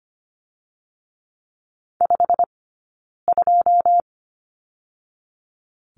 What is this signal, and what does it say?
Morse code, heard as a keyed tone, about 25 wpm, 700 Hz. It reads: 5 2
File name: course / 25:10wpm / 092 Lesson 16d - Letter c - Two Characters 25wpm